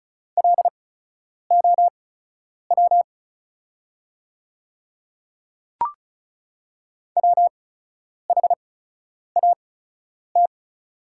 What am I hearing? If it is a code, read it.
Morse code, heard as a keyed tone, about 36 wpm, spaced as for 10 wpm, 690 Hz. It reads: LOW WHAT
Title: 8x Character Spacing - Top 200 Words - 35wpm